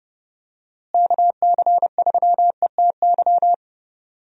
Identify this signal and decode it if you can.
Morse code, heard as a keyed tone, about 32 wpm, 700 Hz. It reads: KC3ETY
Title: USA Call Signs - 6 Characters - Repeated 2x - 30wpm